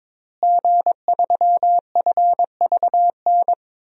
Morse code, heard as a keyed tone, about 22 wpm, 700 Hz. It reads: G3FVN
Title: DXCC-100 Call Signs - 5 Characters - Repeated 3x - 22wpm